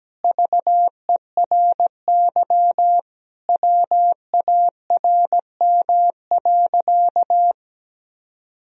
Morse code, heard as a keyed tone, about 17 wpm, 700 Hz. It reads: VERY WARM.
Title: Sentences from Top 1000 Words - Review Part 2 - 17wpm